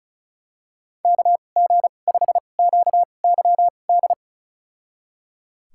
Morse code, heard as a keyed tone, about 36 wpm, spaced as for 25 wpm, 700 Hz. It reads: KG5QYD
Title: USA Call Signs - 6 Characters - 2x Character Spacing - 35wpm